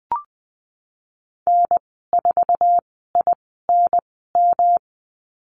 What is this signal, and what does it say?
Morse code, heard as a keyed tone, about 20 wpm, spaced as for 15 wpm, 700 Hz. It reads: N4INM
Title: 186 Lesson 30g - Letter b - Call Signs 20wpm